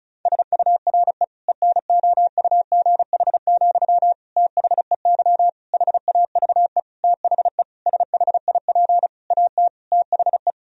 Morse code, heard as a keyed tone, about 36 wpm, 700 Hz. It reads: SURE ENOUGH, THEY HAVE THE SHIP AT THE
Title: Sentences from Top 1000 Words - Review Part 2 - 35wpm